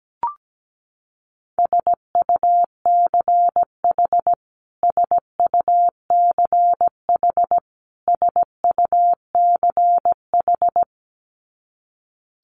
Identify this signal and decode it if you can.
Morse code, heard as a keyed tone, about 17 wpm, 700 Hz. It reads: SUCH SUCH SUCH
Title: Top 500 Words Repeated 17wpm